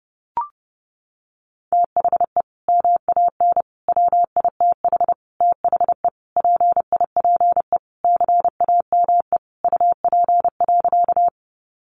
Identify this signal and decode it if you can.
Morse code, heard as a keyed tone, about 30 wpm, 700 Hz. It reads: THE MAN WITH THE PIPE CAME UP.
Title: Sentences from Top 1500 Words 30wpm